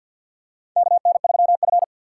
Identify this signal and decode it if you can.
Morse code, heard as a keyed tone, about 51 wpm, 700 Hz. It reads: KN3F
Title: USA Call Signs - 2x Character Spacing - 50wpm